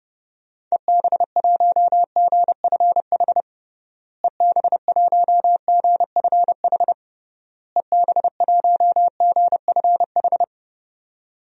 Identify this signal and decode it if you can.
Morse code, heard as a keyed tone, about 30 wpm, 700 Hz. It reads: EB1GFH EB1GFH EB1GFH
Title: DXCC-100 Call Signs - 6 Characters - Repeated 3x - 30wpm